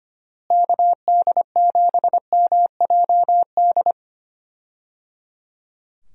Morse code, heard as a keyed tone, about 25 wpm, 700 Hz. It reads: KD7MJD
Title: USA Call Signs - 6 Characters - Repeated 3x - 25wpm